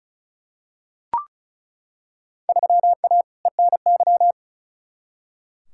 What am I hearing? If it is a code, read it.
Morse code, heard as a keyed tone, about 36 wpm, 700 Hz. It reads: 3A ENY
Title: Field Day Exchanges - 35wpm